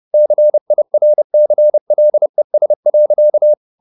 Morse code, heard as a keyed tone, about 30 wpm, 600 Hz. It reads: CIRCLES.